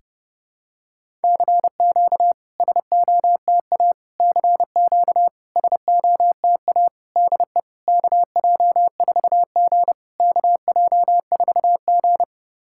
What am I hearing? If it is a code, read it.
Morse code, heard as a keyed tone, about 30 wpm, 690 Hz. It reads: CQ SOTA CQ SOTA DE KJ4G KJ4G
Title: SOTA QSO Practice 30wpm